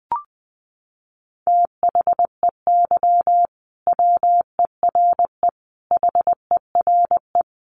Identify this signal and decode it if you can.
Morse code, heard as a keyed tone, about 20 wpm, 700 Hz. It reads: THEY WERE HERE
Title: Sentences from Top 1200 Words 20wpm